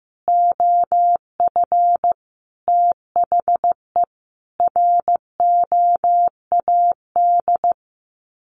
Morse code, heard as a keyed tone, about 15 wpm, 700 Hz. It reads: OF THE ROAD
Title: Sets of 5 Words 15wpm